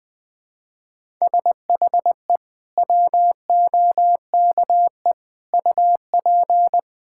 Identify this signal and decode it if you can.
Morse code, heard as a keyed tone, about 20 wpm, 700 Hz. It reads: SHE WOKE UP
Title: Sentences with 6-Word Phrases - 20wpm